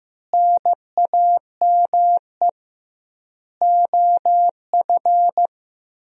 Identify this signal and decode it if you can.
Morse code, heard as a keyed tone, about 16 wpm, 700 Hz. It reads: NAME OF
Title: 2x Word-Spacing - Sentences from Top 300 Words - 15wpm